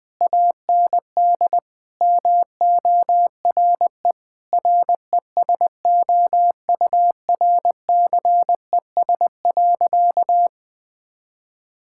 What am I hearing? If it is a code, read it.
Morse code, heard as a keyed tone, about 20 wpm, 700 Hz. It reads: AND MORE RESOURCES.
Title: Sentences with 2-Word Phrases - 20wpm